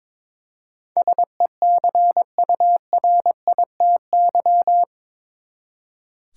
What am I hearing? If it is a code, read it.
Morse code, heard as a keyed tone, about 21 wpm, 700 Hz. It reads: SECURITY
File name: practice / 22wpm / Eight Letter Words - 22wpm